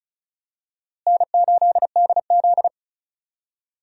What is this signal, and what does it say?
Morse code, heard as a keyed tone, about 36 wpm, 700 Hz. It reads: N8DZ
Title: USA Call Signs - 4 Characters - Repeated 3x - 35wpm